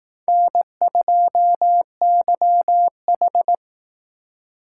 Morse code, heard as a keyed tone, about 18 wpm, 700 Hz. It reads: N2YH
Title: Speed Racing - USA 1x2 Call Signs 15wpm